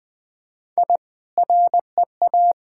Morse code, heard as a keyed tone, about 21 wpm, 700 Hz. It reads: I REA
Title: Sentences with 2-Word Phrases - 20wpm